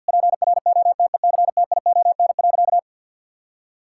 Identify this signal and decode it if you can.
Morse code, heard as a keyed tone, about 50 wpm, 700 Hz. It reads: PROTECTION.
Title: Sentences from Top 1600 Words 50wpm